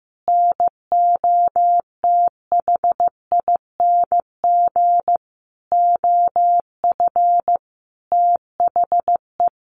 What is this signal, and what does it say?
Morse code, heard as a keyed tone, about 15 wpm, 700 Hz. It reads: NOTHING OF THE
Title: Sentences from Top 1300 Words 15wpm